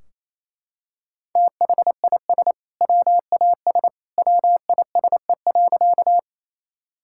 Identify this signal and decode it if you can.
Morse code, heard as a keyed tone, about 28 wpm, 700 Hz. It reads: THIS WAS WISE.
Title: Sentences from Top 1600 Words 28wpm